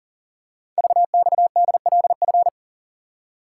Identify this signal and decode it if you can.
Morse code, heard as a keyed tone, about 43 wpm, 700 Hz. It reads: VXDLF
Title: Code Groups - 5 Letters - 40wpm